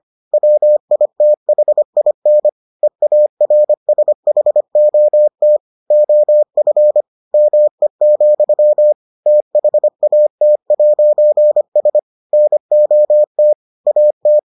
Morse code, heard as a keyed tone, about 25 wpm, 600 Hz. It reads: WITHIN EARSHOT OF ME, THAT'S NOT AT